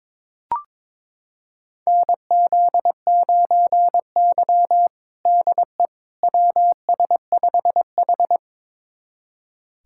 Morse code, heard as a keyed tone, about 22 wpm, 700 Hz. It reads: NZ9Y DE WS5H
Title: USA Callsign DE Callsign - 4 Characters - 22wpm